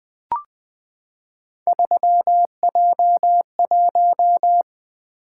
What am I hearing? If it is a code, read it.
Morse code, heard as a keyed tone, about 20 wpm, 700 Hz. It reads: 3J1
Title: Three Letter-Number Combinations - 20wpm